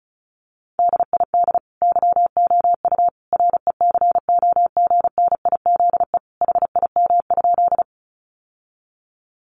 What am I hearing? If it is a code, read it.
Morse code, heard as a keyed tone, about 34 wpm, 700 Hz. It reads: DID YOU RECOGNIZE HIM?
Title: Sentences from Top 1400 Words 35wpm